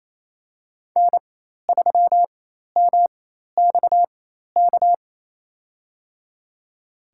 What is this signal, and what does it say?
Morse code, heard as a keyed tone, about 28 wpm, spaced as for 13 wpm, 700 Hz. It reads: N3MXK
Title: USA Call Signs - 5 Characters - 4x Character Spacing - 28wpm